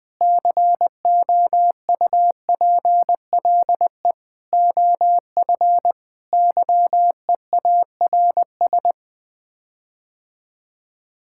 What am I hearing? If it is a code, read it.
Morse code, heard as a keyed tone, about 20 wpm, 700 Hz. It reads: COUPLE OF YEARS
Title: Sets of 6 Words - 20wpm